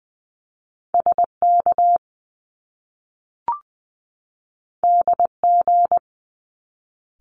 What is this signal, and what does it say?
Morse code, heard as a keyed tone, about 20 wpm, 700 Hz. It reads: SK DG
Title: Top Two Letter Combinations in Words 20wpm